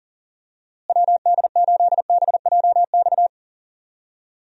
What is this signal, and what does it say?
Morse code, heard as a keyed tone, about 40 wpm, 700 Hz. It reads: WD8BJX
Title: USA Call Signs - 6 Characters - Repeated 3x - 40wpm